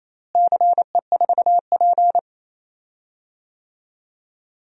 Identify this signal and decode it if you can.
Morse code, heard as a keyed tone, about 28 wpm, 700 Hz. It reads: CE4P
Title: DXCC-100 Call Signs - 4 Characters - Repeated 2x - 28wpm